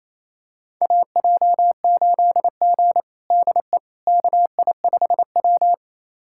Morse code, heard as a keyed tone, about 28 wpm, 700 Hz. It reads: AJ8G DE KI5W
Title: USA Callsign DE Callsign - 4 Characters - 28wpm